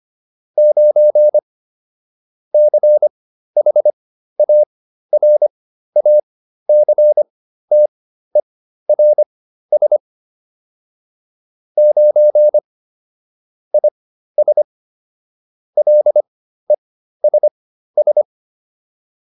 Morse code, heard as a keyed tone, about 25 wpm, spaced as for 13 wpm, 600 Hz. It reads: 9 CHARACTERS 9 IS LESS